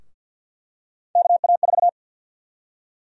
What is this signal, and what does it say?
Morse code, heard as a keyed tone, about 49 wpm, 700 Hz. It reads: KA4
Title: USA Call Sign Prefixes - 50wpm